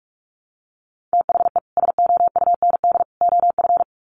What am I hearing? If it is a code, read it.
Morse code, heard as a keyed tone, about 44 wpm, 700 Hz. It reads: THE SOUND OF
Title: Sets of 3 Words 45wpm